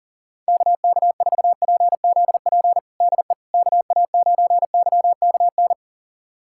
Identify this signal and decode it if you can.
Morse code, heard as a keyed tone, about 40 wpm, 700 Hz. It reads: KK4PZP DE KA9YKN
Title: USA Callsign DE Callsign - 6 Characters - 40wpm